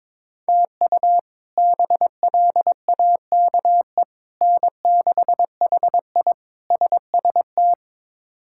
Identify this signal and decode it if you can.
Morse code, heard as a keyed tone, about 22 wpm, 700 Hz. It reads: TU BLAKE N6HI SST
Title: K1USN SST QSO Practice - 22wpm